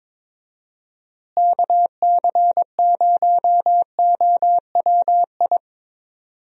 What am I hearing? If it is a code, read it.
Morse code, heard as a keyed tone, about 22 wpm, 700 Hz. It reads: KC0OWI
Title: USA Call Signs - 6 Characters - Repeated 3x - 22wpm